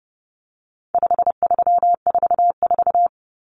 Morse code, heard as a keyed tone, about 30 wpm, 700 Hz. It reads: 5344
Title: Numbers - 4-Digit 30wpm